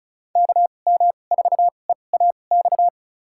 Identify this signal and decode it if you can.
Morse code, heard as a keyed tone, about 36 wpm, spaced as for 25 wpm, 700 Hz. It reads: KM4EAX
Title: USA Call Signs - 6 Characters - 2x Character Spacing - 35wpm